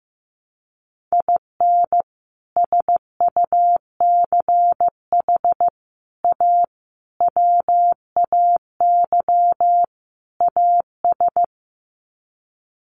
Morse code, heard as a keyed tone, about 14 wpm, 700 Hz. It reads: IN SUCH A WAY AS
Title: Sets of 5 Words Encore 15wpm